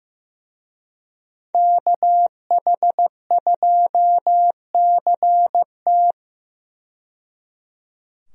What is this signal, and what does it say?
Morse code, heard as a keyed tone, about 15 wpm, 700 Hz. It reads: KH2CT
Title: USA Call Signs - 5 Characters - 15wpm